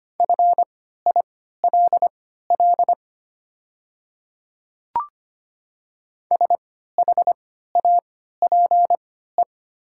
Morse code, heard as a keyed tone, about 25 wpm, spaced as for 14 wpm, 700 Hz. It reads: FILL SHAPE
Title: 3x Character Spacing - Top 500 Words - 25wpm